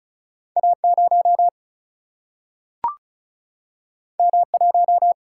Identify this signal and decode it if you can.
Morse code, heard as a keyed tone, about 36 wpm, 700 Hz. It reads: A 0 M 1
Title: Single-Letter Single-Number 35wpm